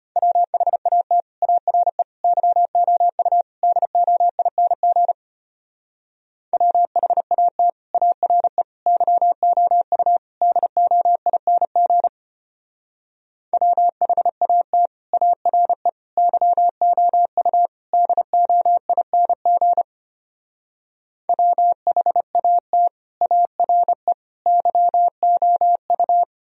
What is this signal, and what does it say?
Morse code, heard as a keyed tone, about 32 wpm, 700 Hz. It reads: WHAT ARE YOU DOING WHAT ARE YOU DOING WHAT ARE YOU DOING WHAT ARE YOU